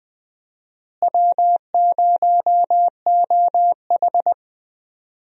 Morse code, heard as a keyed tone, about 20 wpm, 700 Hz. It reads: W0OH
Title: USA Call Signs - 4 Characters - Repeated 3x - 20wpm